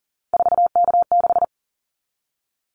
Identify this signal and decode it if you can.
Morse code, heard as a keyed tone, about 39 wpm, 700 Hz. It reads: 4K6
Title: Callsign Prefixes for Top 100 DXCC - 40wpm